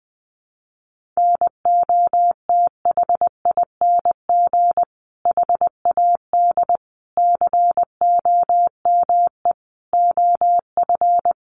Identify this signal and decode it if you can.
Morse code, heard as a keyed tone, about 20 wpm, 700 Hz. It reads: NOTHING HAD COME OF
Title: Sentences from Top 500 Words 20wpm